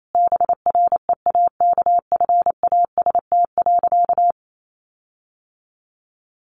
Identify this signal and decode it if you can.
Morse code, heard as a keyed tone, about 28 wpm, 700 Hz. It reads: BREAKFAST.
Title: Sentences from Top 1500 Words 28wpm